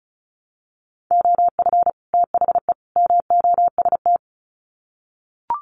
Morse code, heard as a keyed tone, about 34 wpm, 700 Hz. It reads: OF THE MOST
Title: Sets of 3 Words 35wpm